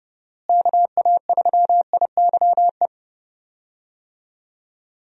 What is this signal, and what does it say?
Morse code, heard as a keyed tone, about 30 wpm, 700 Hz. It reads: KA3IYE
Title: USA Call Signs - 6 Characters - Repeated 2x - 30wpm